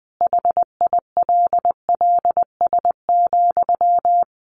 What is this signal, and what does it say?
Morse code, heard as a keyed tone, about 20 wpm, 700 Hz. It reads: HILLS,
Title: Easy Phrases 20wpm